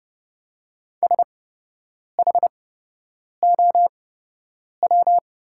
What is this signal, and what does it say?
Morse code, heard as a keyed tone, about 30 wpm, spaced as for 8 wpm, 700 Hz. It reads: SHOW
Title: 8x Character Spacing - Top 300 Words - 30wpm